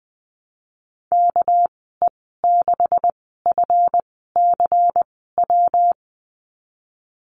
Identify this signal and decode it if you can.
Morse code, heard as a keyed tone, about 20 wpm, spaced as for 15 wpm, 700 Hz. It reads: KE6FCW
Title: 207 Lesson 33g - Letter j - Call Signs 20wpm